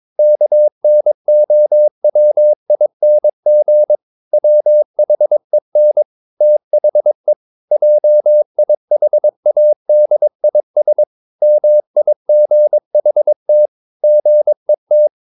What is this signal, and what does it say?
Morse code, heard as a keyed tone, about 22 wpm, 600 Hz. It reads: KNOWING WHEN THE JIHADIS MIGHT GET
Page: Romanen Never af Ken Follet på engelsk.